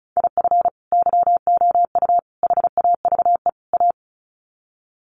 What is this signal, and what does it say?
Morse code, heard as a keyed tone, about 34 wpm, 700 Hz. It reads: IF YOU HAVE A
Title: Sets of 4 Words 35wpm